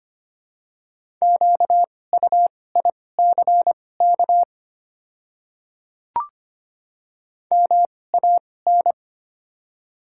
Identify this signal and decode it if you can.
Morse code, heard as a keyed tone, about 25 wpm, spaced as for 18 wpm, 690 Hz. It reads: QUICK MAN
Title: 2x Character Spacing - Top 500 Words - 25wpm